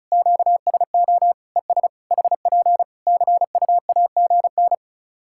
Morse code, heard as a keyed tone, about 36 wpm, 700 Hz. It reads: QSO ES HP CUAGN
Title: Practice QSO - Exchange 4 - Ending - 35wpm